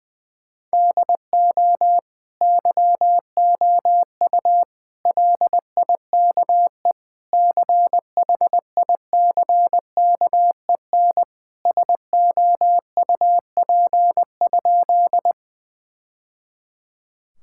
Morse code, heard as a keyed tone, about 20 wpm, 700 Hz. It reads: DO YOU LIKE CHICKEN SOUP?
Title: Sentences from Top 1000 Words - Review Part 6 - 20wpm